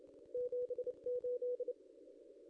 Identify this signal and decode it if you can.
Morse code, heard as a keyed tone, about 26 wpm, 500 Hz. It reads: 78
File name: contests / ss-cw-k5tr-op-n5rz -0700